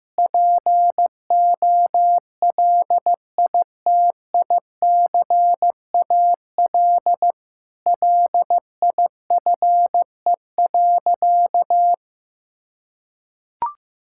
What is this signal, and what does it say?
Morse code, heard as a keyed tone, about 16 wpm, 700 Hz. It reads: POLITICAL LIFE.
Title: Sentences from Top 1000 Words - Review Part 5 - 15wpm